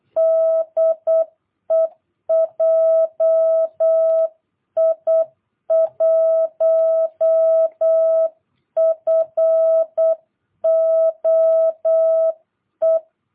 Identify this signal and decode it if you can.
Morse code, heard as a keyed tone, about 8 wpm, 650 Hz. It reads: DE JI1FOE
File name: jh8cbh / 2022cw